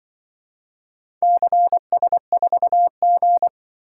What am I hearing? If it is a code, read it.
Morse code, heard as a keyed tone, about 25 wpm, 700 Hz. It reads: CS4G
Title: Speed-Racing - Four Letter-Number Combinations - 20wpm